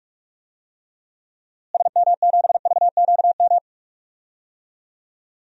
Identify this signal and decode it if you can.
Morse code, heard as a keyed tone, about 45 wpm, 700 Hz. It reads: SM7VQM
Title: DXCC-100 Call Signs - 6 Characters - Repeated 2x - 45wpm